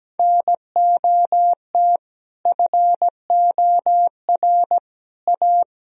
Morse code, heard as a keyed tone, about 17 wpm, 700 Hz. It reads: NOT FOR A
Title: Sentences from Top 1000 Words - Review Part 4 - 17wpm